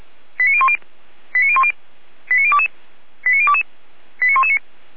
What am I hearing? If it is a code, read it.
Five-tone selective call (ZVEI), heard as quick runs of five tones, tones of about 70 ms. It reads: 89010 89010 89022 89022 89119